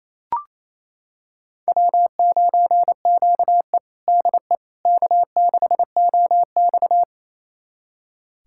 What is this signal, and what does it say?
Morse code, heard as a keyed tone, about 28 wpm, 700 Hz. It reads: W9QE DE K6OX
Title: USA Callsign DE Callsign - 4 Characters - 28wpm